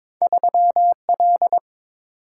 Morse code, heard as a keyed tone, about 23 wpm, 700 Hz. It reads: 3L
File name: practice / 22wpm / Two Letter-Number Combinations - 22wpm